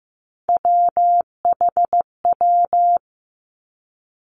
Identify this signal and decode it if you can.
Morse code, heard as a keyed tone, about 15 wpm, 700 Hz. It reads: WHW
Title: Three Challenging Letter-Number Combinations 15wpm